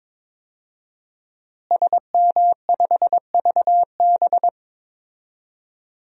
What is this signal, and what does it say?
Morse code, heard as a keyed tone, about 22 wpm, 700 Hz. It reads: SM5VB
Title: DXCC-100 Call Signs - 5 Characters - Repeated 3x - 22wpm